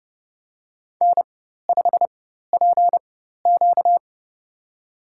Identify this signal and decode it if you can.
Morse code, heard as a keyed tone, about 32 wpm, spaced as for 14 wpm, 700 Hz. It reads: N5PQ
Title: USA Call Signs - 4 Characters - 4x Character Spacing - 30wpm